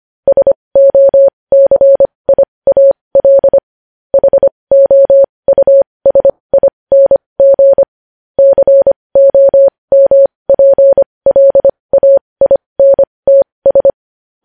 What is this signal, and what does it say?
Morse code, heard as a keyed tone, about 25 wpm, spaced as for 20 wpm, 550 Hz. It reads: SOCIAL HOUSING COMPLAINTS
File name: podcast / News headlines-1x-25x20